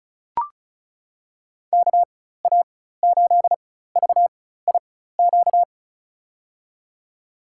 Morse code, heard as a keyed tone, about 36 wpm, spaced as for 16 wpm, 700 Hz. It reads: KA8VIQ
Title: USA Call Signs - 6 Characters - 4x Character Spacing - 35wpm